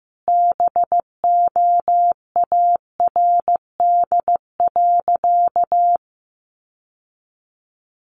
Morse code, heard as a keyed tone, about 14 wpm, 700 Hz. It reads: BOARD.
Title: Sentences from Top 700 Words 15wpm